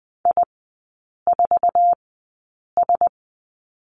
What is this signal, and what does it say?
Morse code, heard as a keyed tone, about 20 wpm, spaced as for 8 wpm, 700 Hz. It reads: I4S
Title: 047 Lesson 9e - Number 4 - Three Characters 20wpm